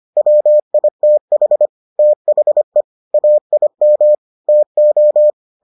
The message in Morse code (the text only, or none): WITH THE AIM TO